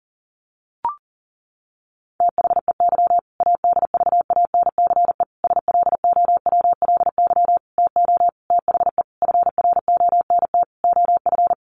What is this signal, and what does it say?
Morse code, heard as a keyed tone, about 40 wpm, 700 Hz. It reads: THEY ADVANCE SLOWLY TO THE FRONT OF
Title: Sentences from Top 1400 Words 40wpm